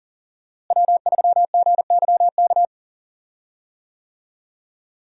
Morse code, heard as a keyed tone, about 40 wpm, 700 Hz. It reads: W3GYK
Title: USA Call Signs - 5 Characters - Repeated 3x - 40wpm